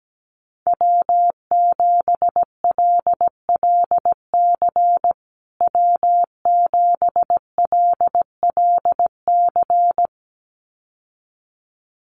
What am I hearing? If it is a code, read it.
Morse code, heard as a keyed tone, about 17 wpm, 700 Hz. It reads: W7LLC W7LLC
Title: USA Call Signs Repeated 2x 17wpm